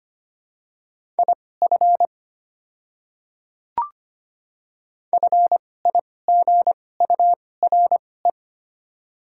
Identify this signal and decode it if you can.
Morse code, heard as a keyed tone, about 25 wpm, spaced as for 18 wpm, 700 Hz. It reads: IF FIGURE
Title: 2x Character Spacing - Top 500 Words - 25wpm